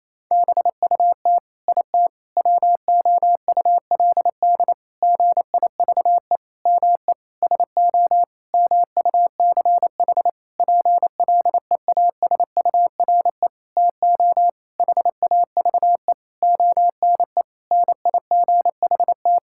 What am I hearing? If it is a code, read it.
Morse code, heard as a keyed tone, about 28 wpm, 700 Hz. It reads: BUT IT WOULD GIVE ME SO MUCH PLEASURE TO HAVE ONE NIGHT
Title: Sentences from Top 1600 Words 28wpm